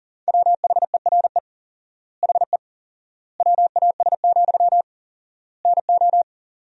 Morse code, heard as a keyed tone, about 40 wpm, 700 Hz. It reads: WHERE HE WAS, NO
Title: 4x Word-Spacing - Sentences from Top 400 Words - 40wpm